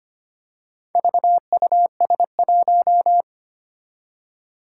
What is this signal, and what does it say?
Morse code, heard as a keyed tone, about 25 wpm, 700 Hz. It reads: VUS1
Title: Four Challenging Letter-Number Combinations 25wpm